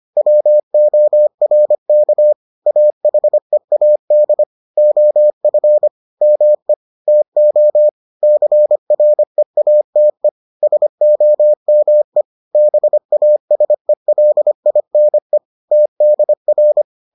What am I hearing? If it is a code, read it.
Morse code, heard as a keyed tone, about 25 wpm, 600 Hz. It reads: WORK AHEAD OF ME TO CREATE SOME BASELINE TDR